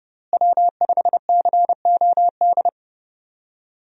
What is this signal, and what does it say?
Morse code, heard as a keyed tone, about 30 wpm, 700 Hz. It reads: W5COD
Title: USA Call Signs - 5 Characters - Repeated 3x - 30wpm